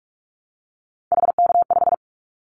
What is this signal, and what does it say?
Morse code, heard as a keyed tone, about 46 wpm, 700 Hz. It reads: HK5
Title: Callsign Prefixes for Top 100 DXCC - 45wpm